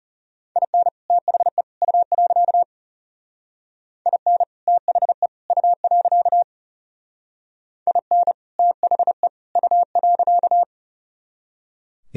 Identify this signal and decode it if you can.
Morse code, heard as a keyed tone, about 36 wpm, 700 Hz. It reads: IN THE U. IN THE U. IN THE U.
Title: Speed-Racing - Sets of 3 Words 30wpm